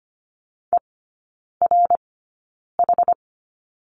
Morse code, heard as a keyed tone, about 25 wpm, spaced as for 9 wpm, 700 Hz. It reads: ERH
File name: course / 25:10wpm / 093 Lesson 16e - Letter c - Three Characters 25wpm